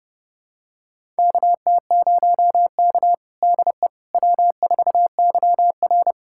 Morse code, heard as a keyed tone, about 30 wpm, 700 Hz. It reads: KT0K DE W4YR
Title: USA Callsign DE Callsign - 4 Characters - 30wpm